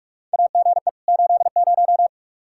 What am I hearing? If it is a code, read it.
Morse code, heard as a keyed tone, about 41 wpm, 700 Hz. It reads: AGE 80
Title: K1USN SST QSO Practice 45wpm